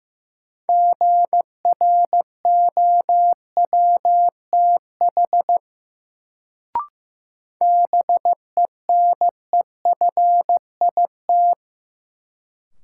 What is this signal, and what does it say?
Morse code, heard as a keyed tone, about 15 wpm, 700 Hz. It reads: GROWTH BENEFIT
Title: Sentences from Top 1300 Words - New Word Drill Alt - 15wpm